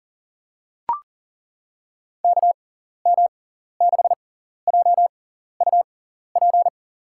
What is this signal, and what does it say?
Morse code, heard as a keyed tone, about 38 wpm, spaced as for 14 wpm, 700 Hz. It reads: KM6JUP
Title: USA Call Signs - 6 Characters - 6x Character Spacing - 40wpm